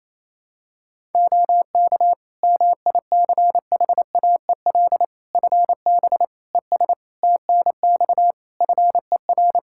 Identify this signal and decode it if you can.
Morse code, heard as a keyed tone, about 28 wpm, 700 Hz. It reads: OK MICHAEL FB ES TNX FER
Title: K1USN SST QSO Practice 28wpm